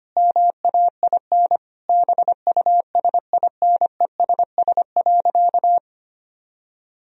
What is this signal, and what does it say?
Morse code, heard as a keyed tone, about 25 wpm, 700 Hz. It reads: MAIN BUSINESS.
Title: Sentences from Top 1000 Words - Review Part 2 - 25wpm